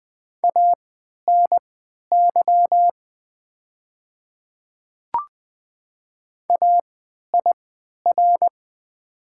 Morse code, heard as a keyed tone, about 20 wpm, spaced as for 11 wpm, 690 Hz. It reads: ANY AIR
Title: 3x Character Spacing - Top 200 Words - 20wpm